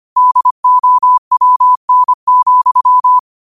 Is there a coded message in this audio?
Yes: Morse code from 0.2 to 3.2 s, heard as a keyed tone, about 25 wpm, 1000 Hz. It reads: DOWN,